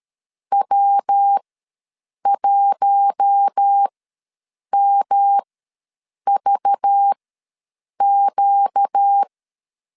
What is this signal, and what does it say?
Morse code, heard as a keyed tone, about 13 wpm, spaced as for 7 wpm, 800 Hz. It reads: W1MVQ
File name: LEZIONI / 5 LBVC / 5 LBV ws 12wpm 3015